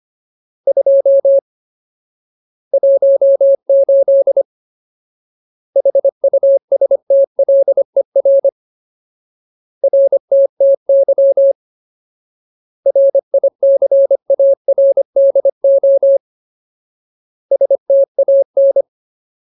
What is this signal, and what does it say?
Morse code, heard as a keyed tone, about 25 wpm, 550 Hz. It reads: 2 18 HUSTLER RTTY RICARDO STAN
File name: Ultimate Word List / 25WPM Ultimate